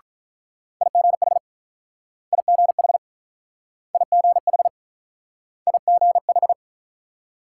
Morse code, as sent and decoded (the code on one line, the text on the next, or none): .. --. .... / .. --. .... / .. --. .... / .. --. ....
IGH IGH IGH IGH